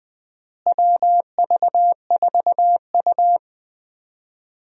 Morse code, heard as a keyed tone, about 20 wpm, 700 Hz. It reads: WV4U
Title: USA Call Signs - 4 Characters - Repeated 3x - 20wpm